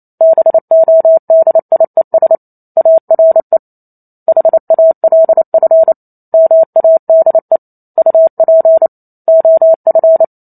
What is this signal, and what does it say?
Morse code, heard as a keyed tone, about 28 wpm, 650 Hz. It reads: BODIES ARE HALF MADE UP OF